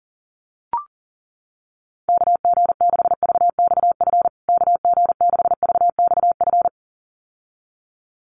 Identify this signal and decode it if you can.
Morse code, heard as a keyed tone, about 40 wpm, 700 Hz. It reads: KG6VXF KG6VXF
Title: USA Call Signs Repeated 2x 40wpm